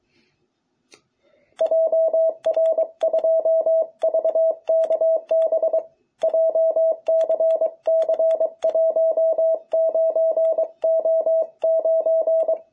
Morse code, heard as a keyed tone, about 21 wpm, 660 Hz. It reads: JR2VKB JCC19O9
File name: jh8cbh / 2021cw